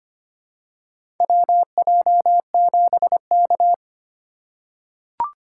Morse code, heard as a keyed tone, about 25 wpm, 700 Hz. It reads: WJ7K
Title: USA Call Signs - 3x Character Spacing - 25wpm